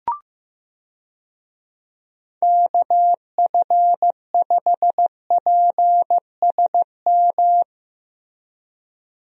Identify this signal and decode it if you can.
Morse code, heard as a keyed tone, about 15 wpm, 700 Hz. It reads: KF5PSM
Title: USA Call Signs - 6 Characters - Repeated 2x - 15wpm